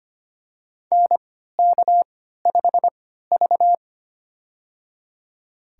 Morse code, heard as a keyed tone, about 25 wpm, spaced as for 14 wpm, 700 Hz. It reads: NK5V
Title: USA Call Signs - 3x Character Spacing - 25wpm